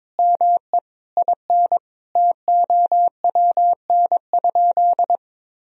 Morse code, heard as a keyed tone, about 22 wpm, 700 Hz. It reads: ME IN TOWN?
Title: Sentences from Top 500 Words and Abbreviations 22wpm